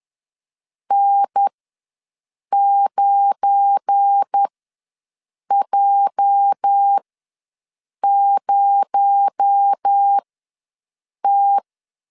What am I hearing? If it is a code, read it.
Morse code, heard as a keyed tone, about 10 wpm, spaced as for 6 wpm, 800 Hz. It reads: N9J0T
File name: LEZIONI / 8 YJ45 / 8 YJ ws 10wpm 4043